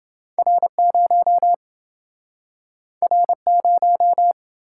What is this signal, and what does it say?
Morse code, heard as a keyed tone, about 28 wpm, 700 Hz. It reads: R0 R0